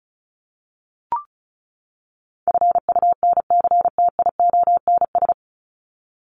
Morse code, heard as a keyed tone, about 34 wpm, 700 Hz. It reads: FUNCTIONS
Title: ES:S Suffix - Top 100 Words with Base Words 35wpm